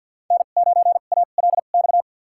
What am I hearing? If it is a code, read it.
Morse code, heard as a keyed tone, about 47 wpm, spaced as for 35 wpm, 700 Hz. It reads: N9ALX
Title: USA Call Signs - 5 Characters - 2x Character Spacing - 50wpm